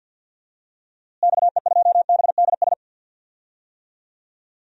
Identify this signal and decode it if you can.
Morse code, heard as a keyed tone, about 51 wpm, 700 Hz. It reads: XE2BDS
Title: DXCC-100 Call Signs - 6 Characters - Repeated 3x - 50wpm